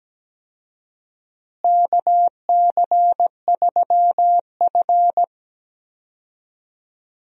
Morse code, heard as a keyed tone, about 17 wpm, 700 Hz. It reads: KC3F
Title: USA Call Signs - 4 Characters - Repeated 3x - 17wpm